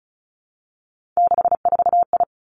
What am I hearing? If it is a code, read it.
Morse code, heard as a keyed tone, about 36 wpm, 700 Hz. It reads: 64I
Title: Three Challenging Letter-Number Combinations 35wpm